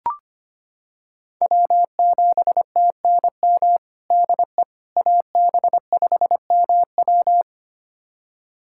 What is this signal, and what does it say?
Morse code, heard as a keyed tone, about 25 wpm, 700 Hz. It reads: W7TNM DE AB5MW
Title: USA Callsign DE Callsign - 5 Characters - 25wpm